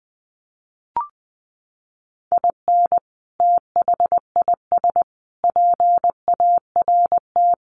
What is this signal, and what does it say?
Morse code, heard as a keyed tone, about 20 wpm, 700 Hz. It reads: IN THIS PART
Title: Sets of 5 Words 20wpm